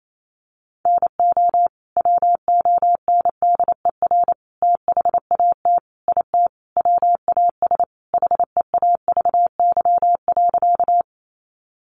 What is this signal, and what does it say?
Morse code, heard as a keyed tone, about 28 wpm, 700 Hz. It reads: NO WONDER THAT IT WAS HEAVY.
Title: Sentences from Top 1100 Words 28wpm